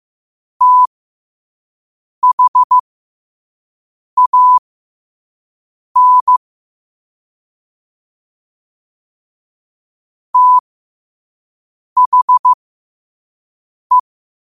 Morse code, heard as a keyed tone, about 15 wpm, spaced as for 5 wpm, 1000 Hz. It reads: THAN THE
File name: qotdfiles / 5wpm